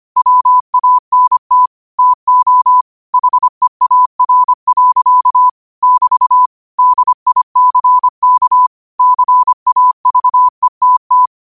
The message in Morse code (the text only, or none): WANT TO HEAR. <BT> DICK CAVETT